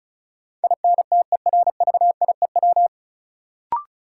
Morse code, heard as a keyed tone, about 36 wpm, 700 Hz. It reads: INTERVIEW
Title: Sentences from Top 1600 Words - New Word Drill Alt - 35wpm